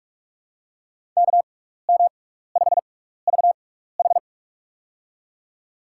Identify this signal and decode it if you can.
Morse code, heard as a keyed tone, about 46 wpm, spaced as for 16 wpm, 700 Hz. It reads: KM5VH
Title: USA Call Signs - 5 Characters - 6x Character Spacing - 45wpm